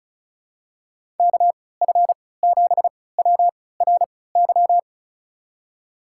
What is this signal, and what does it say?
Morse code, heard as a keyed tone, about 34 wpm, spaced as for 20 wpm, 700 Hz. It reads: KF7WRY
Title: USA Call Signs - 3x Character Spacing - 35wpm